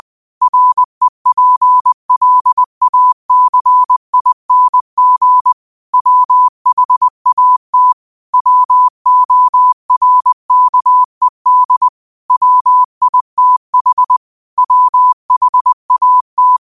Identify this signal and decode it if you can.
Morse code, heard as a keyed tone, about 20 wpm, 1000 Hz. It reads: REPLACING WHAT WORKED WITH WHAT